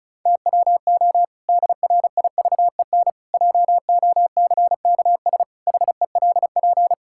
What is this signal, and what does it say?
Morse code, heard as a keyed tone, about 34 wpm, 700 Hz. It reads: TWO DRIVEN JOCKS HELP
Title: Pangram Challenge v2 - 35wpm